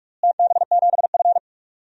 Morse code, heard as a keyed tone, about 43 wpm, 700 Hz. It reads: TB7F
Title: DXCC-100 Call Signs - 4 Characters - Repeated 2x - 45wpm